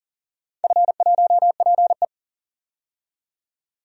Morse code, heard as a keyed tone, about 42 wpm, 700 Hz. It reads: F1PE
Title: DXCC-100 Call Signs - 4 Characters - 40wpm